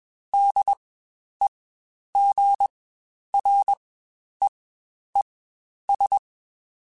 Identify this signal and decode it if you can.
Morse code, heard as a keyed tone, about 21 wpm, spaced as for 10 wpm, 800 Hz. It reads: DEGREES
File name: Spaced Characters / Spaced Characters-mp3s / Weather 4 21 sp @20